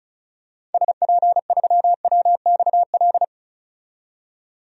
Morse code, heard as a keyed tone, about 34 wpm, 700 Hz. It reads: SP3WXL
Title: DXCC-100 Call Signs - 6 Characters - 35wpm